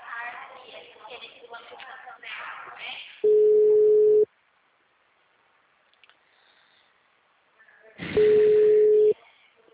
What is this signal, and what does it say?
Ringback tone (Europe, 425 Hz).
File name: out- -2041- - - .47738